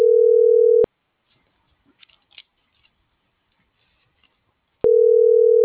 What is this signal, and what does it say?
Ringback tone (North America), 440 + 480 Hz.